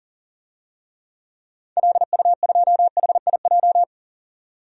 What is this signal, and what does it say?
Morse code, heard as a keyed tone, about 37 wpm, 700 Hz. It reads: LU2HIJ
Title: DXCC-100 Call Signs - 6 Characters - 40wpm